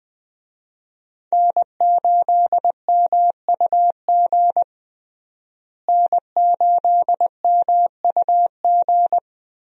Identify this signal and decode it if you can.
Morse code, heard as a keyed tone, about 20 wpm, 700 Hz. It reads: N8MUG N8MUG
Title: USA Call Signs - 5 Characters - Repeated 3x - 20wpm